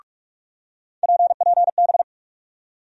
Morse code, heard as a keyed tone, about 42 wpm, 700 Hz. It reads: PPB
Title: USA Call Sign Postfixes - 3 Letters 45wpm